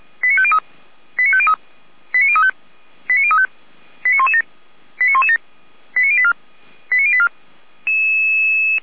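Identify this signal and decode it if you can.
Five-tone selective call (ZVEI), heard as quick runs of five tones, tones of about 70 ms. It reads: 89503 89503 89035 89035 89117 89117 89084 89084
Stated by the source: ZVEI five-tone selective call 89085 received via SDR.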